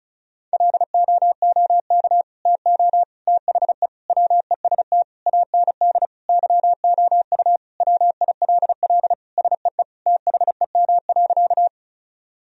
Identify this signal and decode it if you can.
Morse code, heard as a keyed tone, about 36 wpm, 700 Hz. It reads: LOOK TO THE WEST AND YOU WILL SEE THEM.
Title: Sentences from Top 1000 Words - Review Part 2 - 35wpm